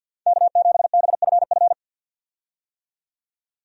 Morse code, heard as a keyed tone, about 51 wpm, 700 Hz. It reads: K7BFF
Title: USA Call Signs - 5 Characters - 50wpm